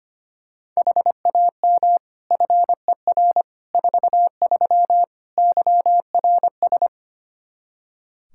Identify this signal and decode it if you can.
Morse code, heard as a keyed tone, about 25 wpm, 700 Hz. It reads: HAM FER 43 YRS
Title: K1USN SST QSO Practice 25wpm